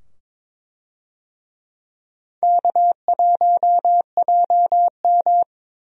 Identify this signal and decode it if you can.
Morse code, heard as a keyed tone, about 22 wpm, 700 Hz. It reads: K1JM
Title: USA Call Signs - 4 Characters - 22wpm